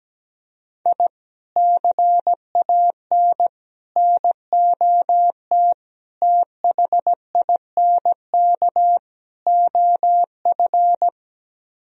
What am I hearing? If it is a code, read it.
Morse code, heard as a keyed tone, about 17 wpm, 700 Hz. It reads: I CAN NOT THINK OF
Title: Speed-Racing - Sets of 5 Words 17wpm